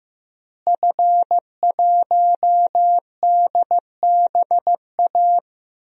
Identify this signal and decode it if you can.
Morse code, heard as a keyed tone, about 15 wpm, 700 Hz. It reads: F1DBA
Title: DXCC-100 Call Signs - 5 Characters - Repeated 3x - 15wpm